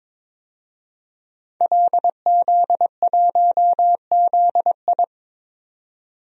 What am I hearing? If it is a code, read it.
Morse code, heard as a keyed tone, about 22 wpm, 700 Hz. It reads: LZ1ZI
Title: DXCC-100 Call Signs - 5 Characters - 22wpm